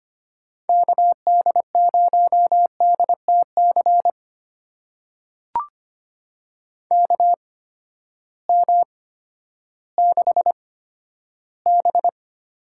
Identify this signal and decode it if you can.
Morse code, heard as a keyed tone, about 25 wpm, spaced as for 12 wpm, 700 Hz. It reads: KD0DTC KM6B
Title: USA Call Signs - 6 Characters - 8x Character Spacing - 25wpm